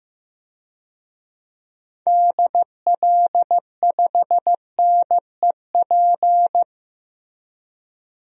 Morse code, heard as a keyed tone, about 15 wpm, 700 Hz. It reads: DL5NEP
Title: DXCC-100 Call Signs - 6 Characters - 15wpm